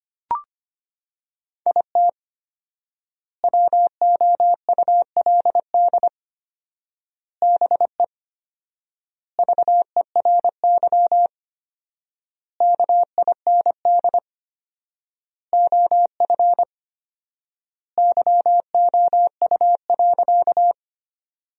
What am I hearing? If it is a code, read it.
Morse code, heard as a keyed tone, about 25 wpm, 700 Hz. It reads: IT WOULD BE VERY KIND OF YOU.
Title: 4x Word-Spacing - Sentences from Top 200 Words - 25wpm